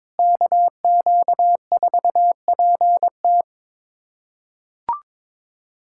Morse code, heard as a keyed tone, about 22 wpm, 700 Hz. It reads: KQ4PT
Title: USA Callsign DE Callsign - 5 Characters - 22wpm